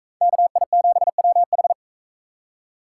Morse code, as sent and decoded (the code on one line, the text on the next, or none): -.- .. --... .-- ....
KI7WH